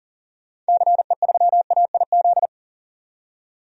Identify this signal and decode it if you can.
Morse code, heard as a keyed tone, about 39 wpm, 700 Hz. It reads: CE3AIZ
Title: DXCC-100 Call Signs - 6 Characters - 40wpm